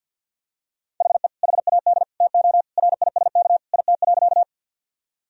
Morse code, heard as a keyed tone, about 50 wpm, 700 Hz. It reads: HE HAD TO RISK IT.
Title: Sentences from Top 1000 Words - Review Part 4 - 50wpm